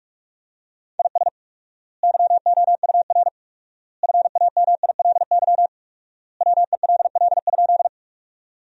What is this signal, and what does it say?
Morse code, heard as a keyed tone, about 45 wpm, 700 Hz. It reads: IS YOUR FAMILY WELL?
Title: 4x Word-Spacing - Sentences from Top 400 Words - 45wpm